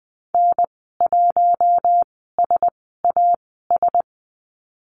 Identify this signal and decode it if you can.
Morse code, heard as a keyed tone, about 20 wpm, spaced as for 14 wpm, 700 Hz. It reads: N1SAS
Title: 042 Lesson 8g - Number 1 - Call Signs 20wpm